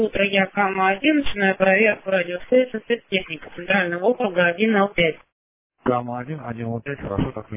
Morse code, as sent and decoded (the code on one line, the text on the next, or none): none